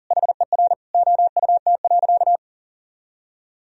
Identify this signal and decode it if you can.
Morse code, heard as a keyed tone, about 40 wpm, 700 Hz. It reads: HER OUT.
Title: Sentences from Top 1000 Words - Review Part 2 - 40wpm